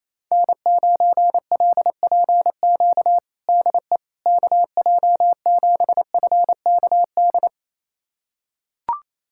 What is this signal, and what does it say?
Morse code, heard as a keyed tone, about 28 wpm, 700 Hz. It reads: N9LPQ DE KJ7FKD
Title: USA Callsign DE Callsign 28wpm